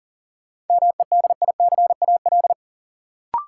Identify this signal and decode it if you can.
Morse code, heard as a keyed tone, about 40 wpm, 700 Hz. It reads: MEDICAL
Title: Sentences from Top 800 Words - New Word Drill - 40wpm